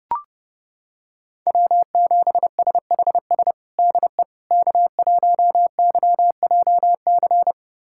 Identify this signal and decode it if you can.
Morse code, heard as a keyed tone, about 30 wpm, 700 Hz. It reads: W7SHS DE K1YJC
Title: USA Callsign DE Callsign - 5 Characters - 30wpm